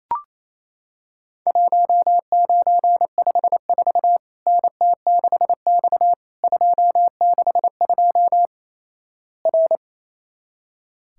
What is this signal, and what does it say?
Morse code, heard as a keyed tone, about 27 wpm, 700 Hz. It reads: 1954 NT6X 262 R
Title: Worked All Europe DX Contest - QTC Exchange - 28wpm